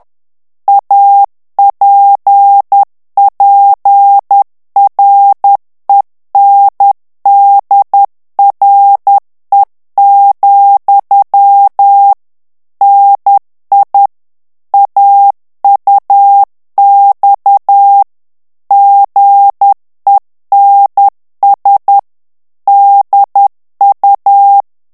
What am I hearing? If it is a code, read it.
Morse code, heard as a keyed tone, about 10 wpm, 800 Hz. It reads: APPRENDRE, NI AUX GENS DU
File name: CW / FICHIERS-MP3 / FRANCAIS TEXTE CLAIR 2 VITESSE 10 MOTS MINUTES